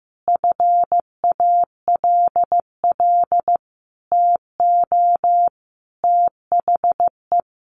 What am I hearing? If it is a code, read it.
Morse code, heard as a keyed tone, about 15 wpm, 700 Hz. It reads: FALL TO THE
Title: Sentences from Top 500 Words 15wpm